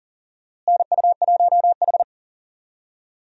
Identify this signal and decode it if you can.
Morse code, heard as a keyed tone, about 40 wpm, 700 Hz. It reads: NU1H
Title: USA Call Signs - 4 Characters - Repeated 3x - 40wpm